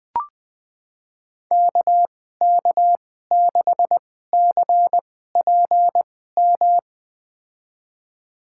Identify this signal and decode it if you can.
Morse code, heard as a keyed tone, about 20 wpm, spaced as for 14 wpm, 700 Hz. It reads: KK6CPM
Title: USA Call Signs - 6 Characters - 2x Character Spacing - 20wpm